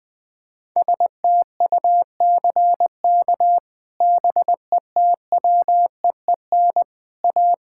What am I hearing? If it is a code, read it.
Morse code, heard as a keyed tone, about 21 wpm, 700 Hz. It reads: STUCK BETWEEN A
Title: Sets of 8 Words - 20wpm